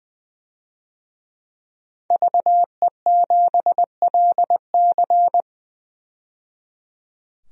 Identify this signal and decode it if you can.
Morse code, heard as a keyed tone, about 21 wpm, 700 Hz. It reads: VE7LC
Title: DXCC-100 Call Signs - 5 Characters - 20wpm